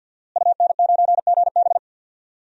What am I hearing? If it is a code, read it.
Morse code, heard as a keyed tone, about 51 wpm, 700 Hz. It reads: UN9GB
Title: DXCC-100 Call Signs - 5 Characters - Repeated 2x - 50wpm